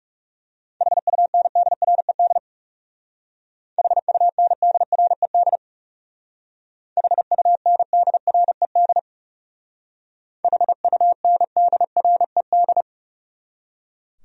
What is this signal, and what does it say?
Morse code, heard as a keyed tone, about 36 wpm, 700 Hz. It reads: HUNDRED HUNDRED HUNDRED HUNDRED
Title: Speed-Racing Top 500 Words 30wpm